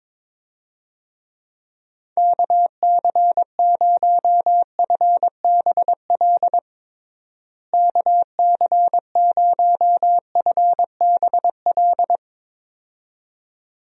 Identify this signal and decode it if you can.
Morse code, heard as a keyed tone, about 21 wpm, 700 Hz. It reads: KC0FBL KC0FBL
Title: USA Call Signs - 6 Characters - Repeated 2x - 22wpm